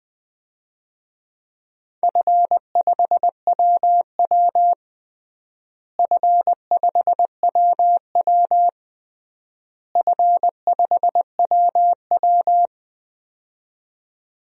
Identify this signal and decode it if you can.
Morse code, heard as a keyed tone, about 20 wpm, 700 Hz. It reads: F5WW F5WW F5WW
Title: DXCC-100 Call Signs - 4 Characters - Repeated 3x - 20wpm